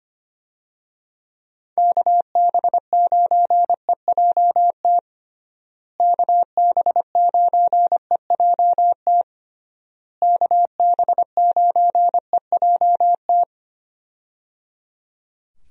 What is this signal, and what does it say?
Morse code, heard as a keyed tone, about 25 wpm, 700 Hz. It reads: KB9EJT KB9EJT KB9EJT
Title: USA Call Signs - 6 Characters - Repeated 3x - 25wpm